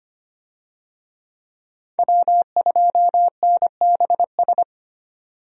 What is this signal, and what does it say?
Morse code, heard as a keyed tone, about 25 wpm, 700 Hz. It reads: W2NBS
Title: USA Call Signs - 5 Characters - 25wpm